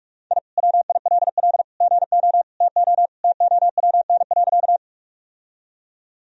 Morse code, heard as a keyed tone, about 46 wpm, 700 Hz. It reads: I WILL GO TO TOWN.
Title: Sentences from Top 1000 Words - Review Part 2 - 45wpm